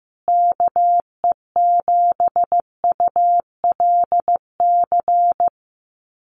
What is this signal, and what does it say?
Morse code, heard as a keyed tone, about 15 wpm, 700 Hz. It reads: KE7ULC
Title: USA Call Signs Repeated 2x 15wpm